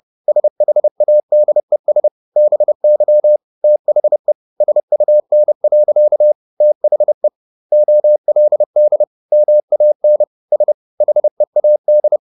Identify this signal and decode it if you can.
Morse code, heard as a keyed tone, about 30 wpm, 600 Hz. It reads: SHADES BY THE SUN. THE OLD MAN S HEAD
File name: the old man and the sea / the old man and the sea 1 30wpm